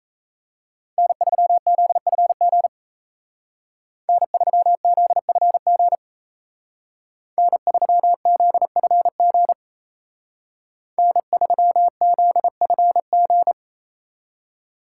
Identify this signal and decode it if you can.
Morse code, heard as a keyed tone, about 33 wpm, 700 Hz. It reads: N3ZFG N3ZFG N3ZFG N3ZFG
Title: Speed Racing - USA 1x3 Call Signs 28wpm